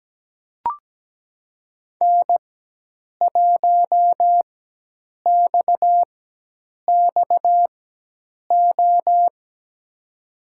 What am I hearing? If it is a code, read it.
Morse code, heard as a keyed tone, about 17 wpm, spaced as for 8 wpm, 700 Hz. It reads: N1XXO
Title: USA Call Signs - 5 Characters - 4x Character Spacing - 17wpm